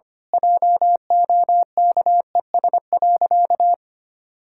Morse code, heard as a keyed tone, about 25 wpm, 700 Hz. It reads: JOKES.
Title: Pangram Challenge v2 - 25wpm